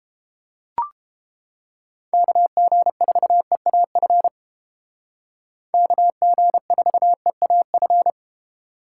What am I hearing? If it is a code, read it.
Morse code, heard as a keyed tone, about 28 wpm, 700 Hz. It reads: KG4EAF KG4EAF